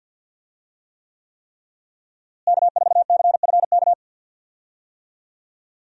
Morse code, heard as a keyed tone, about 51 wpm, 700 Hz. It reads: K4CFK
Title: USA Call Signs - 5 Characters - Repeated 3x - 50wpm